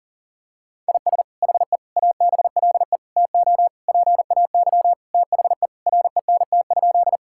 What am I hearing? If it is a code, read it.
Morse code, heard as a keyed tone, about 40 wpm, 700 Hz. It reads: IS HE ABLE TO PAY THE RENT?
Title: Sentences from Top 1000 Words - Review Part 5 - 40wpm